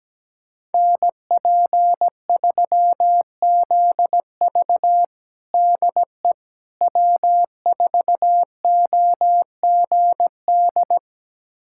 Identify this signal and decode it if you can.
Morse code, heard as a keyed tone, about 17 wpm, 700 Hz. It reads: NP3ZV DE W4OGD
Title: USA Callsign DE Callsign - 5 Characters - 17wpm